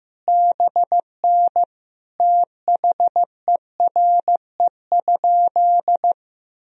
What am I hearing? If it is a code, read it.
Morse code, heard as a keyed tone, about 15 wpm, 700 Hz. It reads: BN THERE?
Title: Sentences from Top 100 Words and Abbreviations 15wpm